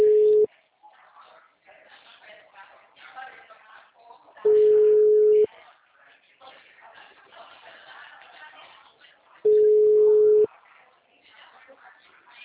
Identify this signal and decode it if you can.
Ringback tone (Europe, 425 Hz), with speech.